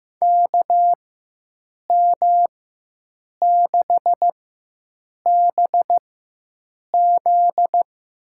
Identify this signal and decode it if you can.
Morse code, heard as a keyed tone, about 15 wpm, spaced as for 7 wpm, 700 Hz. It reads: KM6BZ
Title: USA Call Signs - 4x Character Spacing - 15wpm